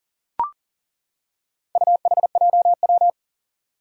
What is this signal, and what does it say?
Morse code, heard as a keyed tone, about 41 wpm, 700 Hz. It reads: UHJW
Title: Four Challenging Letter-Number Combinations 40wpm